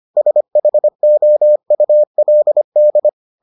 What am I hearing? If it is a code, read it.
Morse code, heard as a keyed tone, about 25 wpm, 600 Hz. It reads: SHOULD